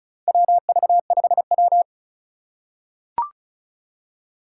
Morse code, heard as a keyed tone, about 34 wpm, 700 Hz. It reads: WV5W
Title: USA Call Signs - 4 Characters - 3x Character Spacing - 35wpm